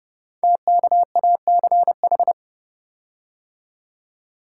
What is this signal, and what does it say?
Morse code, heard as a keyed tone, about 29 wpm, 700 Hz. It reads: TKACH
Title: Code Groups - 5 Letters - 30wpm